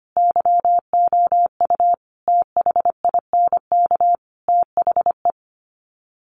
Morse code, heard as a keyed tone, about 25 wpm, 700 Hz. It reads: YOU THINK THE
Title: Sets of 5 Words Encore 25wpm